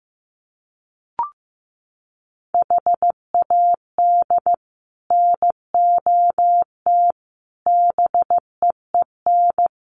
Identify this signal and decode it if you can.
Morse code, heard as a keyed tone, about 15 wpm, 700 Hz. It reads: HAD NOT BEEN
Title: Sets of 5 Words Encore 15wpm